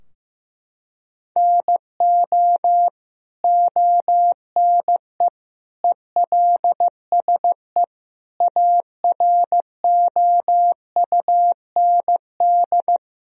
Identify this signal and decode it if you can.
Morse code, heard as a keyed tone, about 15 wpm, 700 Hz. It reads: NO ONE ELSE AROUND
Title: Sentences from Top 1000 Words - Review Part 3 - 15wpm